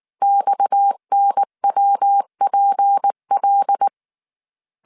Morse code, heard as a keyed tone, about 19 wpm, 800 Hz. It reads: XNWPL
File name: CW / DET / 1 6 1500C S C 18wpm 2356